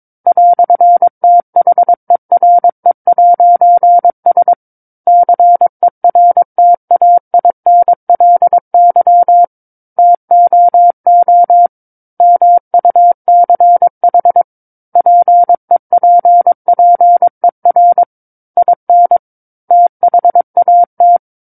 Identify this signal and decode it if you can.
Morse code, heard as a keyed tone, about 22 wpm, 700 Hz. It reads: "THERE'S CERTAINLY TOO MUCH PEPPER IN THAT
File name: 06-Chapter-06 22 WPM